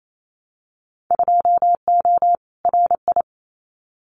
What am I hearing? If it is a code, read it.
Morse code, heard as a keyed tone, about 28 wpm, 700 Hz. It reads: 2O RI
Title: Winter Field Day Exchanges 28wpm